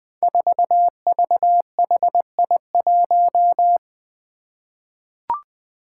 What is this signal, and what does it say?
Morse code, heard as a keyed tone, about 21 wpm, 700 Hz. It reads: 4VHI1
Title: Five Challenging Letter-Number Combinations 20wpm